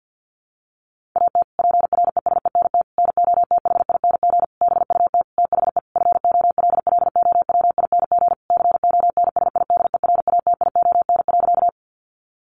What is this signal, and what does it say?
Morse code, heard as a keyed tone, about 50 wpm, 700 Hz. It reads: AT PRESENT NOTHING BUT THE FOLLOWING CONSIDERATION.
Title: Sentences from Top 1500 Words 50wpm